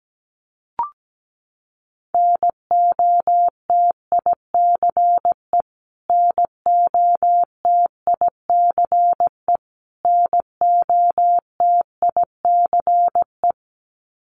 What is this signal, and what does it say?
Morse code, heard as a keyed tone, about 17 wpm, 700 Hz. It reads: NOTICE NOTICE NOTICE
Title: Top 500 Words Repeated 17wpm